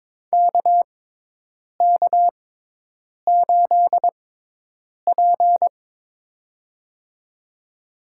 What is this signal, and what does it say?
Morse code, heard as a keyed tone, about 22 wpm, spaced as for 8 wpm, 700 Hz. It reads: KK8P
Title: USA Call Signs - 4 Characters - 6x Character Spacing - 22wpm